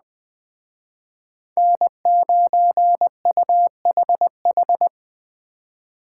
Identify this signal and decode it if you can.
Morse code, heard as a keyed tone, about 20 wpm, 700 Hz. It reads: N9UHH
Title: Speed Racing - USA Call Signs 17wpm